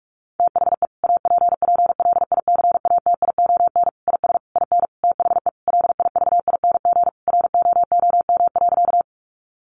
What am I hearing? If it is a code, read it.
Morse code, heard as a keyed tone, about 44 wpm, 700 Hz. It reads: THE APPLICATION IS IN THE LIVING ROOM.
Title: Sentences from Top 1200 Words 45wpm